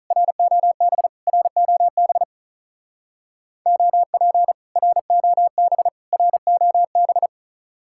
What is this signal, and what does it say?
Morse code, heard as a keyed tone, about 46 wpm, 700 Hz. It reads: ROB ROB OP ROB ROB